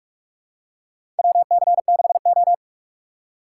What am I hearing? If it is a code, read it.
Morse code, heard as a keyed tone, about 47 wpm, 700 Hz. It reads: WC6O
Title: USA Call Signs - 4 Characters - 6x Character Spacing - 45wpm